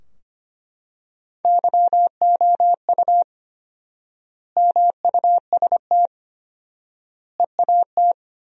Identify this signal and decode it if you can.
Morse code, heard as a keyed tone, about 25 wpm, 700 Hz. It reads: YOU MUST EAT
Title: 4x Word-Spacing - Sentences from Top 300 Words - 25wpm